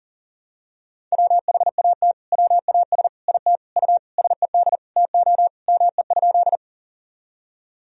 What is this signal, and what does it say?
Morse code, heard as a keyed tone, about 40 wpm, 700 Hz. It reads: WHAT WAS IT U SED TO ME?
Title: Sentences from Top 200 Words and Abbreviations 40wpm